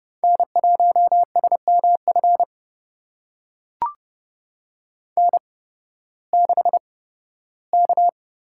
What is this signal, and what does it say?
Morse code, heard as a keyed tone, about 30 wpm, spaced as for 16 wpm, 700 Hz. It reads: N1SMF N6K
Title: USA Call Signs - 5 Characters - 8x Character Spacing - 30wpm